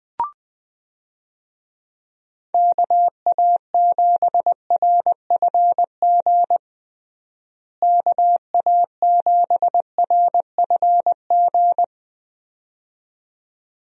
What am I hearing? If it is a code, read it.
Morse code, heard as a keyed tone, about 21 wpm, 700 Hz. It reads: KA7RFG KA7RFG
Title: USA Call Signs - 6 Characters - Repeated 2x - 20wpm